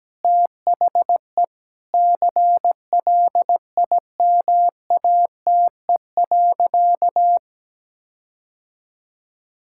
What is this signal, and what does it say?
Morse code, heard as a keyed tone, about 17 wpm, 700 Hz. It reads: THE CLIMATE.
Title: Sentences from Top 1600 Words 17wpm